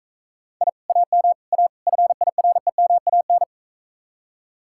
Morse code, heard as a keyed tone, about 43 wpm, 700 Hz. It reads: I AM A FIREMAN